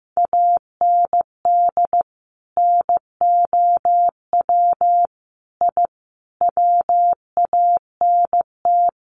Morse code, heard as a keyed tone, about 15 wpm, 700 Hz. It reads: AND NOW I WANT
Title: A Sound Of Thunder 15wpm